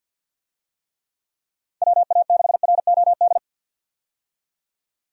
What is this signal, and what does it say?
Morse code, heard as a keyed tone, about 50 wpm, 700 Hz. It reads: WA6ROD
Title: USA Call Signs - 6 Characters - 50wpm